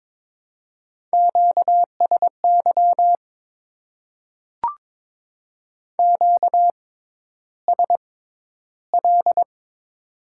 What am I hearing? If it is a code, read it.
Morse code, heard as a keyed tone, about 22 wpm, spaced as for 11 wpm, 700 Hz. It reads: QSY QSL
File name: Top 90 QSO Elements - 6x Character Spacing - 22wpm